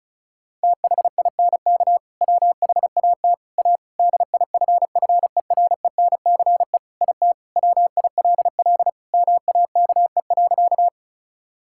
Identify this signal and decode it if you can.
Morse code, heard as a keyed tone, about 34 wpm, 700 Hz. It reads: THINK WHAT A DIFFERENCE IT WILL MAKE.
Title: Sentences from Top 1000 Words - Review Part 5 - 35wpm